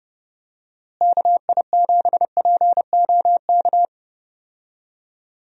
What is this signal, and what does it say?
Morse code, heard as a keyed tone, about 30 wpm, 700 Hz. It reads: KI7POK
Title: USA Call Signs - 6 Characters - Repeated 3x - 30wpm